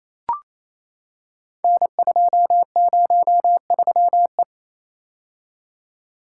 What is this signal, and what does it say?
Morse code, heard as a keyed tone, about 29 wpm, 700 Hz. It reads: N203E
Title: Code Groups - 5 Letter:Numbers - 28wpm